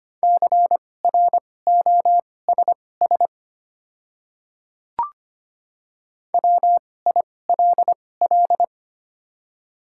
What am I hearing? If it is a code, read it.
Morse code, heard as a keyed tone, about 25 wpm, spaced as for 18 wpm, 700 Hz. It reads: CROSS WILL
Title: 2x Character Spacing - Top 500 Words - 25wpm